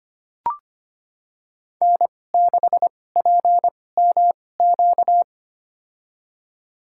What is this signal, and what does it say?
Morse code, heard as a keyed tone, about 25 wpm, spaced as for 18 wpm, 700 Hz. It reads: N6PMQ
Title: USA Call Signs - 5 Characters - 2x Character Spacing - 25wpm